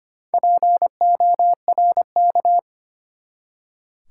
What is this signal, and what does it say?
Morse code, heard as a keyed tone, about 25 wpm, 690 Hz. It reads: PORK
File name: practice / 25wpm / Easy Words with Hints - 25wpm